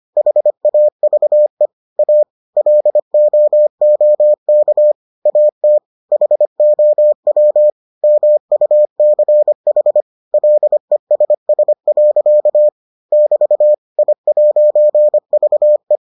Morse code, heard as a keyed tone, about 25 wpm, 600 Hz. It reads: HAVE A LOOK AT HOW MUCH LESS. <BT> I'VE